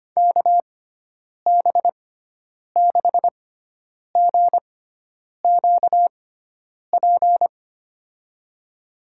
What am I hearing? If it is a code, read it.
Morse code, heard as a keyed tone, about 25 wpm, spaced as for 9 wpm, 700 Hz. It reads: KB6GQP
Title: USA Call Signs - 6 Characters - 6x Character Spacing - 25wpm